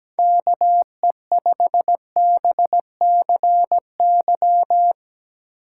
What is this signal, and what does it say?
Morse code, heard as a keyed tone, about 17 wpm, 700 Hz. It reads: KE5BCY
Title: K1USN SST QSO Practice - 17wpm